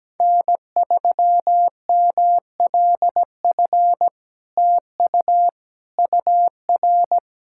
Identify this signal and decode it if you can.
Morse code, heard as a keyed tone, about 17 wpm, 700 Hz. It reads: N3MLF TU UR
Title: SOTA QSO Practice 17wpm